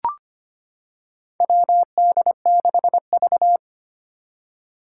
Morse code, heard as a keyed tone, about 25 wpm, 700 Hz. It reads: WD6V
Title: Four Challenging Letter-Number Combinations 25wpm